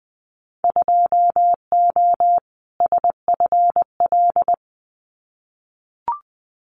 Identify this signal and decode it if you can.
Morse code, heard as a keyed tone, about 20 wpm, 700 Hz. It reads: 2O SFL
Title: Winter Field Day Exchanges 20wpm